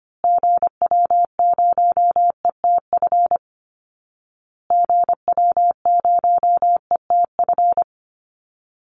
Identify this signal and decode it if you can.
Morse code, heard as a keyed tone, about 25 wpm, 700 Hz. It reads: GW0ETF GW0ETF
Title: Call Signs Repeated 2x - 25wpm